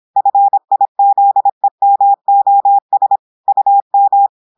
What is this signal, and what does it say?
Morse code, heard as a keyed tone, about 26 wpm, 800 Hz. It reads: FIZEMOS UM
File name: audios / portugais / portugais texte en clair 1 vitesse 26 mots minutes